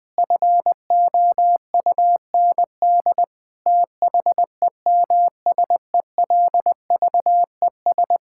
Morse code, heard as a keyed tone, about 21 wpm, 700 Hz. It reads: FOUND THEMSELVES
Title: Sentences from Top 1000 Words - Review Part 5 - 20wpm